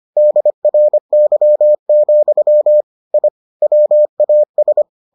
Morse code, heard as a keyed tone, about 25 wpm, 600 Hz. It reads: DRY, I WAS